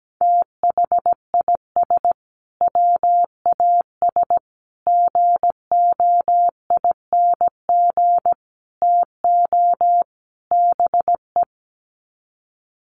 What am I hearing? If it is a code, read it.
Morse code, heard as a keyed tone, about 17 wpm, 700 Hz. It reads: THIS WAS GOING TO BE
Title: Sets of 5 Words Encore2 17wpm